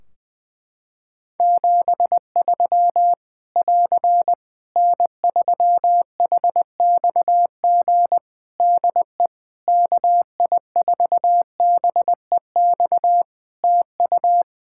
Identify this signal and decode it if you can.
Morse code, heard as a keyed tone, about 20 wpm, 700 Hz. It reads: 73 <AR> N3HXG DE KI4BEX TU
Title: K1USN SST QSO Practice 20wpm